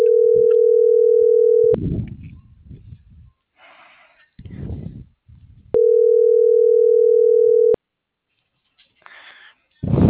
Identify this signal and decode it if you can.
Ringback tone (North America), 440 + 480 Hz.